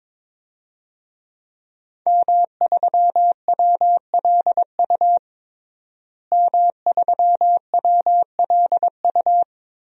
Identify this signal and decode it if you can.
Morse code, heard as a keyed tone, about 22 wpm, 700 Hz. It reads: M3WLU M3WLU
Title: DXCC-100 Call Signs - 5 Characters - Repeated 3x - 22wpm